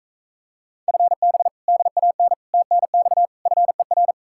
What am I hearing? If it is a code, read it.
Morse code, heard as a keyed tone, about 45 wpm, 700 Hz. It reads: FB DAN TNX FER